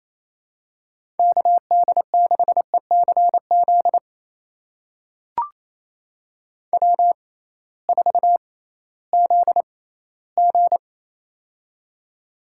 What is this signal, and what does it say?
Morse code, heard as a keyed tone, about 28 wpm, spaced as for 16 wpm, 700 Hz. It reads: KD6ECZ W4ZG
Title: USA Call Signs - 6x Character Spacing - 28wpm